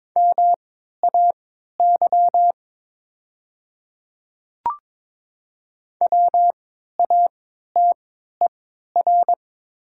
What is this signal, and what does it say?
Morse code, heard as a keyed tone, about 22 wpm, spaced as for 12 wpm, 700 Hz. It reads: MAY WATER
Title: 3x Character Spacing - Top 100 Words - 22wpm